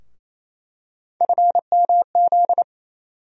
Morse code, heard as a keyed tone, about 27 wpm, 700 Hz. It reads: FMZ
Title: USA Call Sign Postfixes - 3 Letters 28wpm